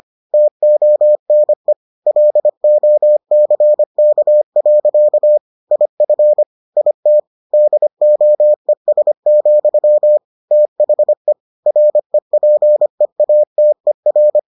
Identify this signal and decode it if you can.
Morse code, heard as a keyed tone, about 25 wpm, 600 Hz. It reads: TONE LOCK. IF IT DOES, THE REPEATER